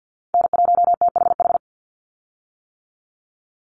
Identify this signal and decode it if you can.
Morse code, heard as a keyed tone, about 50 wpm, 700 Hz. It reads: N1THH
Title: USA Call Signs 50wpm